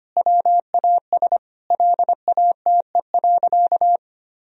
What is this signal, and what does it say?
Morse code, heard as a keyed tone, about 25 wpm, 700 Hz. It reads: WAS LATE.
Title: Sentences from Top 1000 Words - Review Part 3 - 25wpm